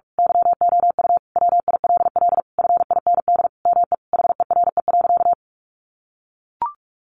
Morse code, heard as a keyed tone, about 44 wpm, 700 Hz. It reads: YOU WILL FIND ME HERE.
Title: Sentences from Top 1200 Words 45wpm